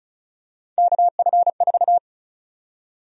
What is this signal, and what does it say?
Morse code, heard as a keyed tone, about 36 wpm, 700 Hz. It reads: KF4
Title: USA Call Sign Prefixes - 35wpm